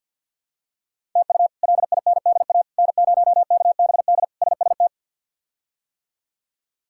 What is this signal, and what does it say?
Morse code, heard as a keyed tone, about 52 wpm, 700 Hz. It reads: TU LINDA N0KBD SST
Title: K1USN SST QSO Practice - 50wpm